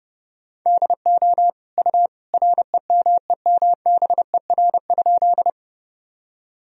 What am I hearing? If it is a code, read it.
Morse code, heard as a keyed tone, about 30 wpm, 700 Hz. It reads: DO U REMEMBER?
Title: Sentences from Top 400 Words and Abbreviations 30wpm